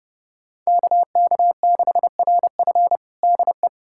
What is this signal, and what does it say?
Morse code, heard as a keyed tone, about 30 wpm, 700 Hz. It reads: KK6RF DE
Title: USA Callsign DE Callsign - 5 Characters - 30wpm